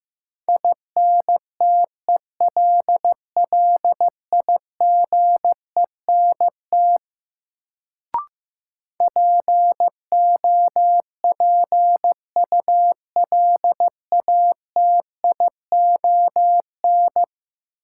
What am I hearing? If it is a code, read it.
Morse code, heard as a keyed tone, about 15 wpm, 700 Hz. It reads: INTELLIGENT POPULATION
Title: Sentences from Top 1000 Words - Review Part 7 - New Word Drill - 15wpm